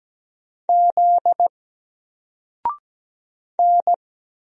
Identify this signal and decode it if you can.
Morse code, heard as a keyed tone, about 17 wpm, 700 Hz. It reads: Z N
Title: Single Character (Letter:Number:Punctuation:Prosign) 17wpm